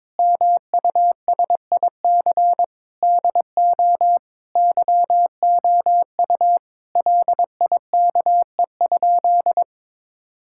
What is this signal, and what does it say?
Morse code, heard as a keyed tone, about 22 wpm, 700 Hz. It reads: MUSIC DO YOU LIKE?
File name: practice / 22wpm / Sentences with 3-Word Phrases - 22wpm